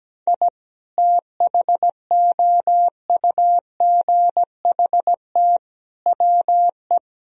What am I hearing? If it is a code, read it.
Morse code, heard as a keyed tone, about 17 wpm, 700 Hz. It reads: I THOUGHT WE
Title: Sentences from Top 1000 Words - Review Part 2 - 17wpm